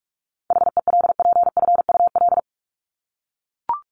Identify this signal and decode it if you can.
Morse code, heard as a keyed tone, about 44 wpm, 700 Hz. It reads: HELPFUL
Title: Top 751-1000 Words 45wpm